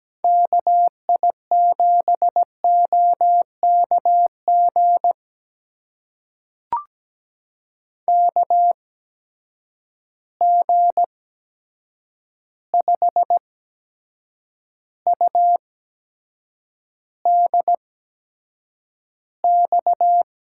Morse code, heard as a keyed tone, about 17 wpm, spaced as for 7 wpm, 700 Hz. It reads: KI7OKG KG5UDX
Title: USA Call Signs - 6 Characters - 8x Character Spacing - 17wpm